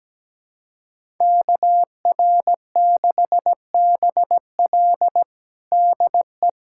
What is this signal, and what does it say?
Morse code, heard as a keyed tone, about 17 wpm, 700 Hz. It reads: KR6BL DE
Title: USA Callsign DE Callsign - 5 Characters - 17wpm